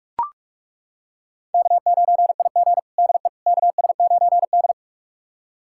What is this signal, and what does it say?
Morse code, heard as a keyed tone, about 41 wpm, 700 Hz. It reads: K9IG DE KS9D
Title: USA Callsign DE Callsign - 4 Characters - 45wpm